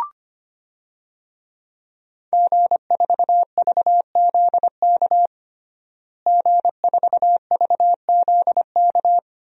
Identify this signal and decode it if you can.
Morse code, heard as a keyed tone, about 25 wpm, 700 Hz. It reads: G4VZK G4VZK
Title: DXCC-100 Call Signs - 5 Characters - Repeated 2x - 25wpm